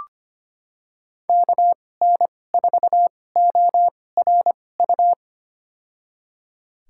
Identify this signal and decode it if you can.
Morse code, heard as a keyed tone, about 25 wpm, spaced as for 18 wpm, 700 Hz. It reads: KN4ORU
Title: USA Call Signs - 6 Characters - 2x Character Spacing - 25wpm